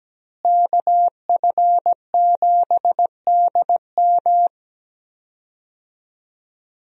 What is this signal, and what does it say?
Morse code, heard as a keyed tone, about 17 wpm, 700 Hz. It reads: KF7DM
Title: USA Call Signs - 5 Characters - Repeated 3x - 17wpm